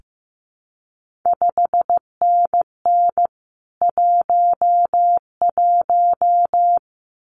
Morse code, heard as a keyed tone, about 15 wpm, 700 Hz. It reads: 5NN 11
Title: CQ WW DX Contest Exchange Encore - 15wpm